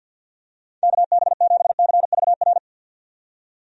Morse code, heard as a keyed tone, about 51 wpm, 700 Hz. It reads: KB7CVR
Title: USA Call Signs - 6 Characters - 50wpm